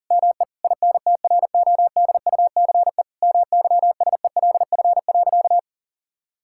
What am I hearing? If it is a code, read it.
Morse code, heard as a keyed tone, about 40 wpm, 700 Hz. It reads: ME INTRODUCE MYSELF.
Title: Sentences from Top 1600 Words 40wpm